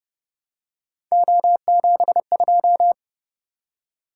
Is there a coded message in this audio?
Yes: Morse code from 1.1 to 2.9 s, heard as a keyed tone, about 30 wpm, 700 Hz. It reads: O72